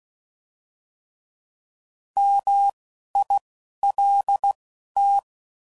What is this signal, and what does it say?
Morse code, heard as a keyed tone, about 16 wpm, spaced as for 11 wpm, 800 Hz. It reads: MILT